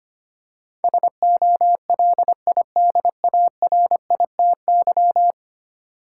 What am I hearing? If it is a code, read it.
Morse code, heard as a keyed tone, about 25 wpm, 700 Hz. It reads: SOLIDARITY
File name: 10 Letter Words - First 3 Letters Given - 25wpm